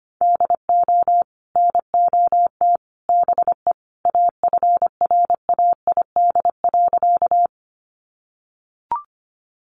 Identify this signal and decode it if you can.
Morse code, heard as a keyed tone, about 25 wpm, 700 Hz. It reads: DO NOT BE AFRAID.
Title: Sentences from Top 1500 Words 25wpm